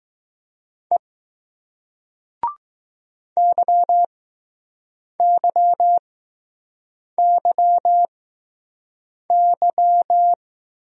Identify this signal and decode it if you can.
Morse code, heard as a keyed tone, about 19 wpm, spaced as for 6 wpm, 700 Hz. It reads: E YYYY